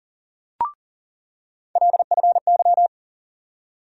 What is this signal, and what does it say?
Morse code, heard as a keyed tone, about 37 wpm, 700 Hz. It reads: LFY
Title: USA Call Sign Postfixes - 3 Letters 40wpm